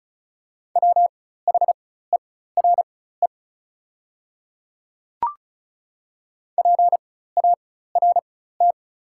Morse code, heard as a keyed tone, about 34 wpm, spaced as for 16 wpm, 690 Hz. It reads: WHERE PART
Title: 4x Character Spacing - Top 200 Words - 35wpm